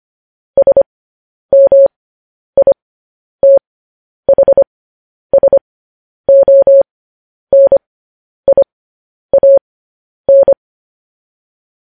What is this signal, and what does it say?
Morse code, heard as a keyed tone, about 25 wpm, spaced as for 10 wpm, 550 Hz. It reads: SMITHSONIAN
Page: news-headlines-in-morse-code-at-10-wpm